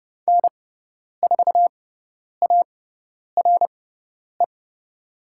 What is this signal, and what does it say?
Morse code, heard as a keyed tone, about 30 wpm, spaced as for 10 wpm, 700 Hz. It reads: N4ARE
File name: Lesson 16g - Letter c - Call Signs 30wpm